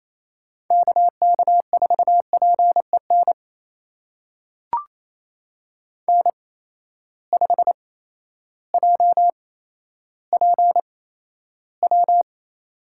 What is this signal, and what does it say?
Morse code, heard as a keyed tone, about 28 wpm, spaced as for 13 wpm, 700 Hz. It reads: KK4PEN N5JPW
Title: USA Call Signs - 8x Character Spacing - 28wpm